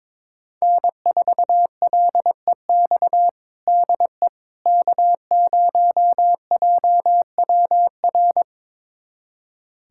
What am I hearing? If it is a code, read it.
Morse code, heard as a keyed tone, about 22 wpm, 700 Hz. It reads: N4LEX DE K0JWR
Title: USA Callsign DE Callsign - 5 Characters - 22wpm